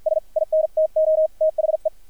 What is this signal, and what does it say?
Morse code, heard as a keyed tone, about 53 wpm, spaced as for 45 wpm, 630 Hz, with radio noise. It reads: S INTO THE
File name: archived / CRSnet